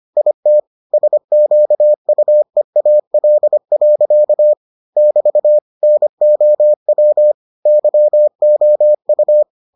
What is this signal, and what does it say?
Morse code, heard as a keyed tone, about 25 wpm, 600 Hz. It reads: IT SQUEAL. <BT> NOW YOU